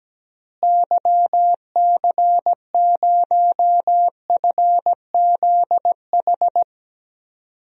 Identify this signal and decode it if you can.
Morse code, heard as a keyed tone, about 17 wpm, 700 Hz. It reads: YC0FZH
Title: DXCC-100 Call Signs - 6 Characters - Repeated 2x - 17wpm